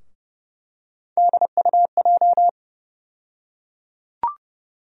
Morse code, heard as a keyed tone, about 30 wpm, 700 Hz. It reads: DUJ
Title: USA Call Sign Postfixes - 3 Letters 30wpm